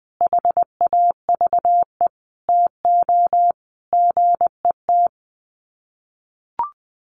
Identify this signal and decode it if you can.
Morse code, heard as a keyed tone, about 20 wpm, 700 Hz. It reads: HAVE TO GET
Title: Sets of 5 Words Encore2 20wpm